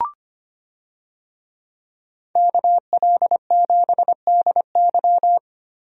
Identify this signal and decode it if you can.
Morse code, heard as a keyed tone, about 25 wpm, 700 Hz. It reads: KL7DY
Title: DXCC-100 Call Signs - 5 Characters - Repeated 2x - 25wpm